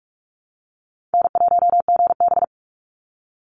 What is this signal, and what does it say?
Morse code, heard as a keyed tone, about 44 wpm, 700 Hz. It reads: N1GB
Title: NA SOTA Chaser Callsigns 45wpm